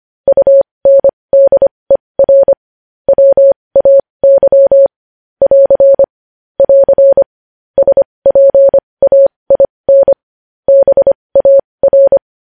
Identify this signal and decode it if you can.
Morse code, heard as a keyed tone, about 25 wpm, spaced as for 20 wpm, 550 Hz. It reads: UNDER WAY <AR> <AR> SPAIN BAR
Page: news-headlines-in-morse-code-at-20-wpm